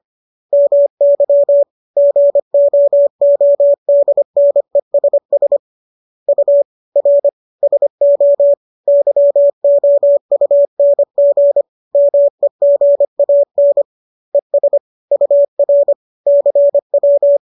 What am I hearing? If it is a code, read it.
Morse code, heard as a keyed tone, about 25 wpm, 570 Hz. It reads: MY GOODNESS U R SO YOUNG MEGAN ES UR CW